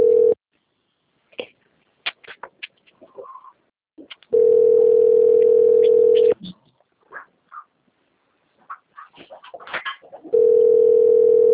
Ringback tone (North America), 440 + 480 Hz.
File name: out- - - - - .3864